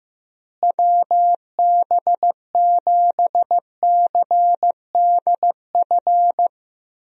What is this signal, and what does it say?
Morse code, heard as a keyed tone, about 16 wpm, 700 Hz. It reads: WB7CDF
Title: USA Call Signs - 6 Characters - Repeated 3x - 15wpm